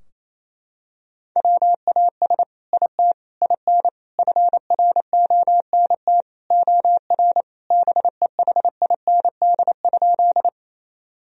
Morse code, heard as a keyed tone, about 27 wpm, 700 Hz. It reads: WAS IT IN FRONT OR BEHIND?
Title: Sentences from Top 1000 Words - Review Part 2 - 28wpm